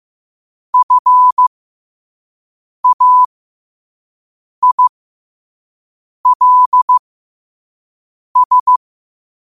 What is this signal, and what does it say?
Morse code, heard as a keyed tone, about 15 wpm, spaced as for 5 wpm, 1000 Hz. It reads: FAILS
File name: qotdfiles / 5wpm